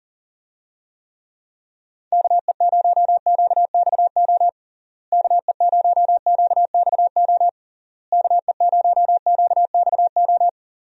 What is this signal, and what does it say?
Morse code, heard as a keyed tone, about 40 wpm, 700 Hz. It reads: KE0QXO KE0QXO KE0QXO
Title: USA Call Signs - 6 Characters - Repeated 3x - 40wpm